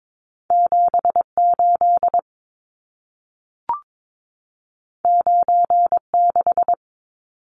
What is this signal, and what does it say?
Morse code, heard as a keyed tone, about 22 wpm, 700 Hz. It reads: 78 96
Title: Numbers - 2-Digit 22wpm